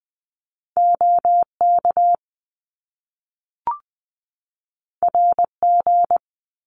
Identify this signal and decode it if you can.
Morse code, heard as a keyed tone, about 20 wpm, 700 Hz. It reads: OK RG
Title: Top Two Letter Combinations in Words 20wpm